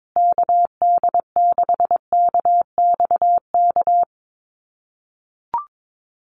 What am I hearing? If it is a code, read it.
Morse code, heard as a keyed tone, about 22 wpm, 700 Hz. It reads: KD6KXK
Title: USA Call Signs Repeated 2x 22wpm